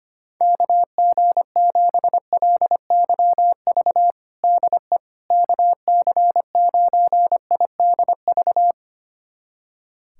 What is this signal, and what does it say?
Morse code, heard as a keyed tone, about 25 wpm, 700 Hz. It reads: KG7LYV DE KC9IDV
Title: USA Callsign DE Callsign - 6 Characters - 25wpm